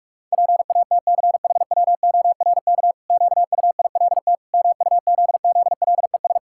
Morse code, heard as a keyed tone, about 41 wpm, 700 Hz. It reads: PATCHWORK QUILT MUZZLES
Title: Pangram Challenge v2 - 45wpm